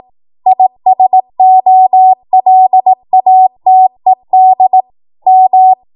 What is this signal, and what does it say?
Morse code, heard as a keyed tone, about 18 wpm, 750 Hz. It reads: ISOLATED M
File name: Morse / 18WPM